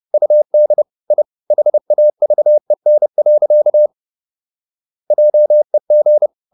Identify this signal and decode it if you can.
Morse code, heard as a keyed tone, about 30 wpm, 600 Hz. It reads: UD I HAVEN. JEG
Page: Ferie 30 wpm